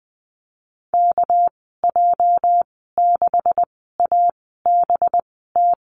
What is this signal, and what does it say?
Morse code, heard as a keyed tone, about 20 wpm, spaced as for 14 wpm, 700 Hz. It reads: KJ6ABT
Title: 238 Lesson 38g - Letter q - Call Signs 20wpm